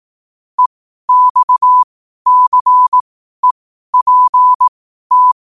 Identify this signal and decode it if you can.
Morse code, heard as a keyed tone, about 18 wpm, spaced as for 12 wpm, 1000 Hz. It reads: EXCEPT
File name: qotdfiles / 12wpm